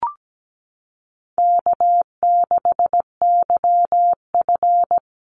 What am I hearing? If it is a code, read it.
Morse code, heard as a keyed tone, about 17 wpm, 700 Hz. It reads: K6YF
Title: Easy Call Signs Repeated 3x - 17wpm